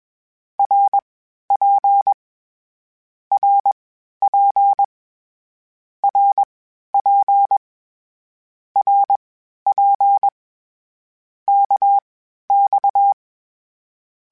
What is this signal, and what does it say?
Morse code, heard as a keyed tone, about 21 wpm, spaced as for 12 wpm, 800 Hz. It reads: RP RP RP RP KX
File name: Short Runs / Short Runs-mp3s / Balanced @20